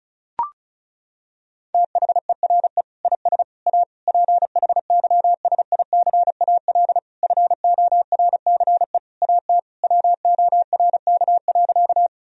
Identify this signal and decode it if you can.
Morse code, heard as a keyed tone, about 36 wpm, 700 Hz. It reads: THERE IS A PHYSICAL FORCE AT WORK.
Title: Sentences from Top 1000 Words - Review Part 4 - 35wpm